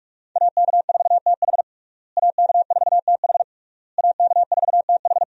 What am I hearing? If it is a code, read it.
Morse code, heard as a keyed tone, about 46 wpm, 700 Hz. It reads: AK4TH AK4TH AK4TH
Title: USA Call Signs - 5 Characters - Repeated 3x - 45wpm